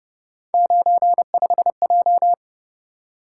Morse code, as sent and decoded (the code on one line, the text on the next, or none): ----. ..... .---
95J